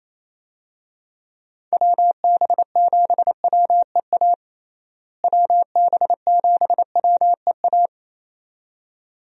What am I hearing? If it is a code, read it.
Morse code, heard as a keyed tone, about 28 wpm, 700 Hz. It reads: WB7WEA WB7WEA
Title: USA Call Signs - 6 Characters - Repeated 2x - 28wpm